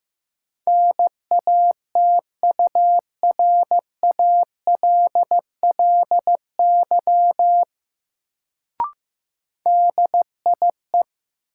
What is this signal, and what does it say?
Morse code, heard as a keyed tone, about 16 wpm, 700 Hz. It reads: NATURALLY DIE
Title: Sentences from Top 1300 Words - New Word Drill - 15wpm